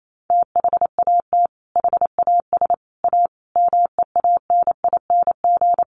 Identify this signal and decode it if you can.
Morse code, heard as a keyed tone, about 28 wpm, 700 Hz. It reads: THAT HAS A MEANING
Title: Sentences from Top 900 Words 28wpm